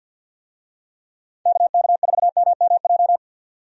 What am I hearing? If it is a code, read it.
Morse code, heard as a keyed tone, about 51 wpm, 700 Hz. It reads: KK4MMJ
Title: USA Call Signs - 6 Characters - 50wpm